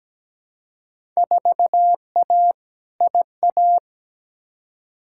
Morse code, heard as a keyed tone, about 17 wpm, 700 Hz. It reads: 4A IA
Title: Field Day Exchanges - 17wpm